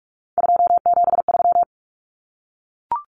Morse code, heard as a keyed tone, about 44 wpm, 700 Hz. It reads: 273
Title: Numbers - 3-Digit 45wpm